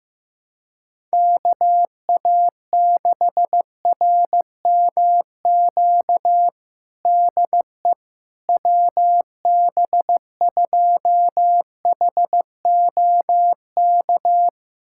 Morse code, heard as a keyed tone, about 15 wpm, 700 Hz. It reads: KA6RMQ DE WB2HOK
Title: USA Callsign DE Callsign - 6 Characters - 15wpm